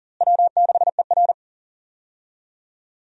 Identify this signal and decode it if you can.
Morse code, heard as a keyed tone, about 40 wpm, 700 Hz. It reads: W6ER
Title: USA Call Signs - 4 Characters - Repeated 2x - 40wpm